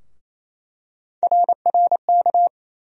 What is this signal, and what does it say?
Morse code, heard as a keyed tone, about 28 wpm, 700 Hz. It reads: RRK
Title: USA Call Sign Postfixes - 3 Letters 28wpm